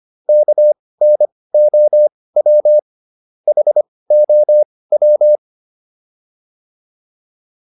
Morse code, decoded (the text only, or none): KNOW HOW